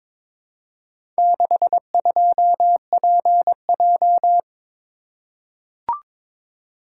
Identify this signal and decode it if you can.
Morse code, heard as a keyed tone, about 22 wpm, 700 Hz. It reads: 62PJ
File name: practice / 22wpm / Four Letter-Number Combinations - 22wpm